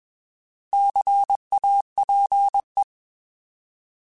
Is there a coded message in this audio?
Yes: Morse code from 0.7 to 2.8 s, heard as a keyed tone, about 21 wpm, 800 Hz. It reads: CAPE